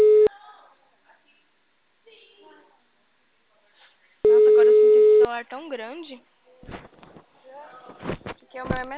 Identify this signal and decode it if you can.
Ringback tone (Europe, 425 Hz), with speech.
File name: out- -2025- - - .56